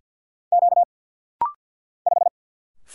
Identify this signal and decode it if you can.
Morse code, heard as a keyed tone, about 50 wpm, 690 Hz. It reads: Q 5
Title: Single Letter-Number Alt - Rapid-Fire 50wpm